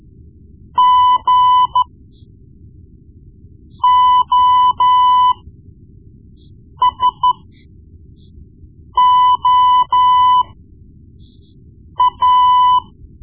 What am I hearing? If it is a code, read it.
Morse code, heard as a keyed tone, about 13 wpm, spaced as for 4 wpm, 1000 Hz. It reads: GOSOA
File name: Morse Lessons / Morse Lessons / Tape-1B